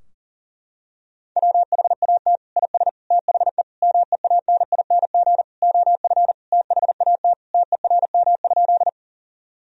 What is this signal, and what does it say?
Morse code, heard as a keyed tone, about 40 wpm, 700 Hz. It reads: WHAT IS THE MEANING OF THAT TERM?
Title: Sentences from Top 1000 Words - Review Part 4 - 40wpm